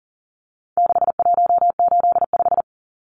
Morse code, heard as a keyed tone, about 40 wpm, 700 Hz. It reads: 6185
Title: Numbers - 4-Digit 40wpm